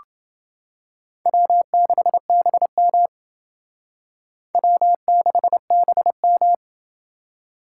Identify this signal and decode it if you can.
Morse code, heard as a keyed tone, about 29 wpm, 700 Hz. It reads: W6BM W6BM